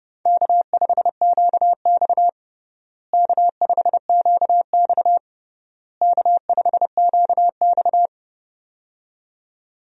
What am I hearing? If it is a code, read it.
Morse code, heard as a keyed tone, about 32 wpm, 700 Hz. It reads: K5QX K5QX K5QX
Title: USA Call Signs - 4 Characters - Repeated 3x - 30wpm